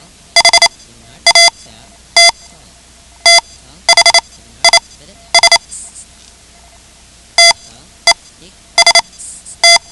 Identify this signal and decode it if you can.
Morse code, heard as a keyed tone, about 28 wpm, spaced as for 12 wpm, 2050 Hz. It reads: HAT THIS TEST
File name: ham radio / CodeQuick / Code Quick 10